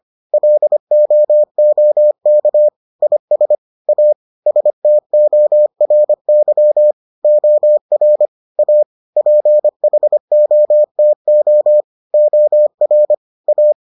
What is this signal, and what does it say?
Morse code, heard as a keyed tone, about 25 wpm, 600 Hz. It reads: LOOK IS A STORY OR A PHOTO OR A